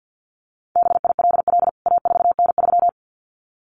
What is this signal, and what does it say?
Morse code, heard as a keyed tone, about 50 wpm, 700 Hz. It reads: BILL A4N3
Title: CWT Exchange 50wpm